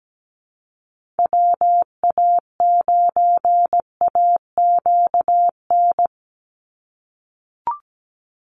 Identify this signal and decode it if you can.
Morse code, heard as a keyed tone, about 17 wpm, 700 Hz. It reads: WA9AQN
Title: Call Signs - 17wpm